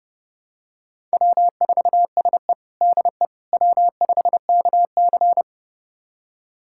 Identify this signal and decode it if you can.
Morse code, heard as a keyed tone, about 32 wpm, 700 Hz. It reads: W4SE DE W5KC
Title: USA Callsign DE Callsign - 4 Characters - 30wpm